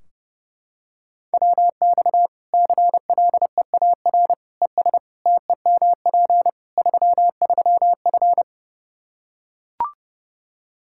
Morse code, heard as a keyed tone, about 30 wpm, 700 Hz. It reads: WX CLEAR ES TEMP 33F
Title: Practice QSO - Exchange 2 - Info - 30wpm